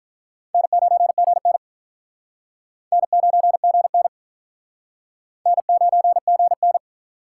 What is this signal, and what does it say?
Morse code, heard as a keyed tone, about 44 wpm, 700 Hz. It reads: N9GN N9GN N9GN